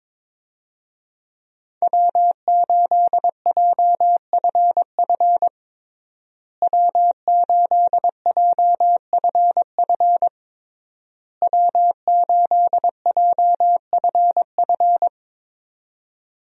Morse code, heard as a keyed tone, about 22 wpm, 700 Hz. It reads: W8JFF W8JFF W8JFF
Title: USA Call Signs - 5 Characters - Repeated 3x - 22wpm